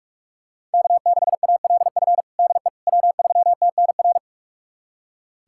Morse code, heard as a keyed tone, about 41 wpm, 700 Hz. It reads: K6ALF DE W3TNR
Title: USA Callsign DE Callsign - 5 Characters - 45wpm